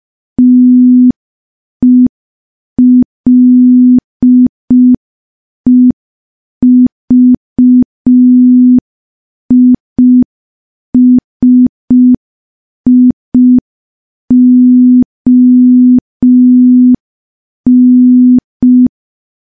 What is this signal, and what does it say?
Morse code, heard as a keyed tone, about 5 wpm, 250 Hz. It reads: TELEVISION
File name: morse / morse-training-05WPM